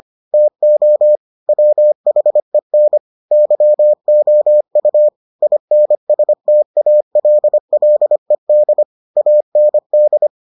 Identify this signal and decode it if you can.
Morse code, heard as a keyed tone, about 25 wpm, 600 Hz. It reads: TO WHEN YOU INSTALLED AND